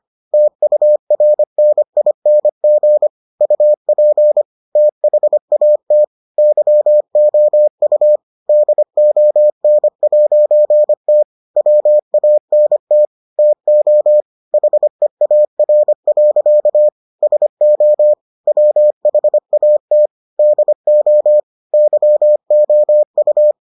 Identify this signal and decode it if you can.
Morse code, heard as a keyed tone, about 25 wpm, 600 Hz. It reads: TURNING UP THAT YOU DON'T WANT TO HEAR. SO WHAT DO YOU